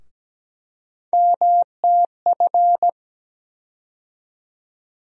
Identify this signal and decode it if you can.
Morse code, heard as a keyed tone, about 17 wpm, 700 Hz. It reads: MTF
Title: USA Call Sign Postfixes - 3 Letters 17wpm